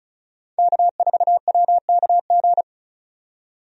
Morse code, heard as a keyed tone, about 34 wpm, 700 Hz. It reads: K4WKG
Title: USA Call Signs - 5 Characters - 35wpm